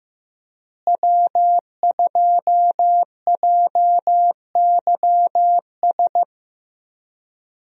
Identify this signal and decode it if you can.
Morse code, heard as a keyed tone, about 15 wpm, 700 Hz. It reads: W2JYS
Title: USA Call Signs - 5 Characters - Repeated 2x - 15wpm